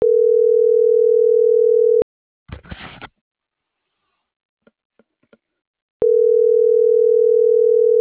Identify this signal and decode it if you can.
Ringback tone (North America), 440 + 480 Hz.